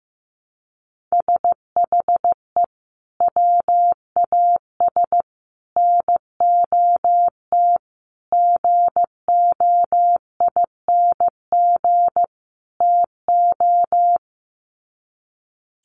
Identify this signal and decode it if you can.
Morse code, heard as a keyed tone, about 15 wpm, 700 Hz. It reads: SHE WAS NOT GOING TO
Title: Sets of 5 Words Encore2 15wpm